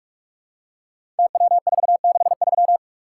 Morse code, heard as a keyed tone, about 44 wpm, 700 Hz. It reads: TW463
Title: Code Groups - 5 Letter:Numbers - 45wpm